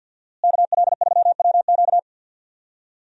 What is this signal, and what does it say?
Morse code, heard as a keyed tone, about 50 wpm, 700 Hz. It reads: KL3WQ
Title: USA Call Signs - 5 Characters - Repeated 3x - 50wpm